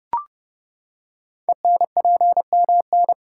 Morse code, heard as a keyed tone, about 29 wpm, 700 Hz. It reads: ENPMN
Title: Code Groups - 5 Letter:Numbers - 30wpm